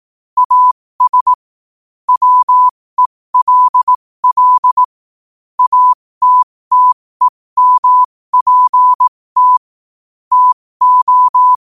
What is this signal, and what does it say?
Morse code, heard as a keyed tone, about 18 wpm, spaced as for 15 wpm, 1000 Hz. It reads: AS WELL ATTEMPT TO
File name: qotdfiles / 15wpm